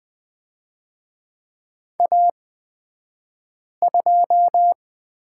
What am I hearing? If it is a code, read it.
Morse code, heard as a keyed tone, about 20 wpm, spaced as for 5 wpm, 700 Hz. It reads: A2
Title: 249 Lesson 40d - BK - Two Characters 20wpm